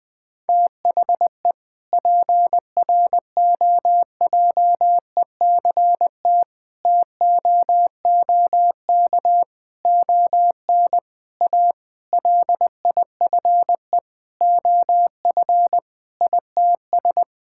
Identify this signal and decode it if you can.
Morse code, heard as a keyed tone, about 21 wpm, 700 Hz. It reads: THE PROJECT TOOK ON A LIFE OF ITS
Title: Sentences with 7-Word Phrases - 20wpm